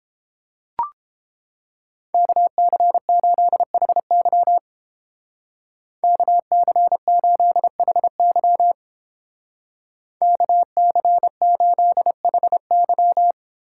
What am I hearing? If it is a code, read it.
Morse code, heard as a keyed tone, about 29 wpm, 700 Hz. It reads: KC8HY KC8HY KC8HY